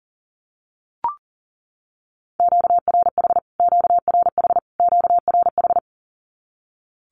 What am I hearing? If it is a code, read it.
Morse code, heard as a keyed tone, about 40 wpm, 700 Hz. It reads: QRH QRH QRH
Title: Q-Codes Repeated 3x 40wpm